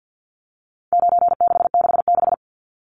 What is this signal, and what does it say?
Morse code, heard as a keyed tone, about 50 wpm, 700 Hz. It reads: 9666
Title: Numbers - 4-Digit 50wpm